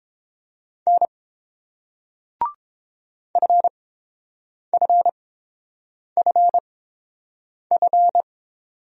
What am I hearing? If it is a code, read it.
Morse code, heard as a keyed tone, about 29 wpm, spaced as for 7 wpm, 700 Hz. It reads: N FFFF